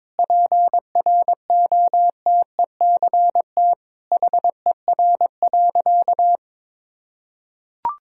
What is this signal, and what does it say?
Morse code, heard as a keyed tone, about 22 wpm, 700 Hz. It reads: PROTECT HER.
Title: Sentences from Top 1600 Words 22wpm